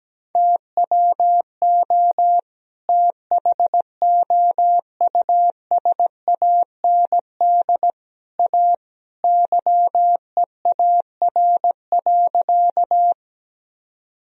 Morse code, heard as a keyed tone, about 17 wpm, 700 Hz. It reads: TWO THOUSAND A YEAR.
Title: Sentences from Top 1000 Words - Review Part 3 - 17wpm